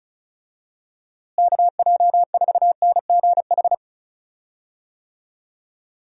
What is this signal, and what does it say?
Morse code, heard as a keyed tone, about 34 wpm, 700 Hz. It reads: KJ4NGH
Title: USA Call Signs - 6 Characters - Repeated 2x - 35wpm